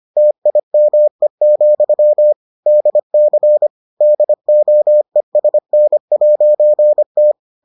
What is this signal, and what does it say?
Morse code, heard as a keyed tone, about 25 wpm, 600 Hz. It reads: TIME, DC DOESN'T